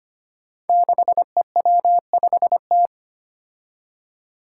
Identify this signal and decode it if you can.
Morse code, heard as a keyed tone, about 25 wpm, 700 Hz. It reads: 6EW5T
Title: Five Challenging Letter-Number Combinations 25wpm